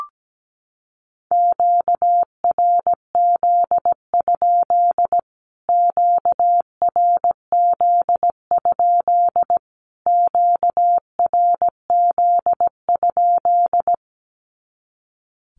Morse code, heard as a keyed tone, about 17 wpm, 700 Hz. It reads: QRZ? QRZ? QRZ?
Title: Q-Codes Repeated 3x 17wpm